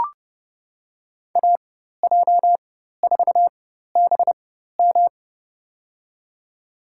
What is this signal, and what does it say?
Morse code, heard as a keyed tone, about 30 wpm, spaced as for 14 wpm, 700 Hz. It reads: AJ4BM
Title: USA Call Signs - 5 Characters - 4x Character Spacing - 30wpm